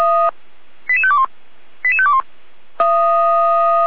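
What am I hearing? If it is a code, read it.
Five-tone selective call (ZVEI), heard as quick runs of five tones, tones of about 70 ms. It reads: 88521 88521
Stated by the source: Five-tone ZVEI selective call received on SDR.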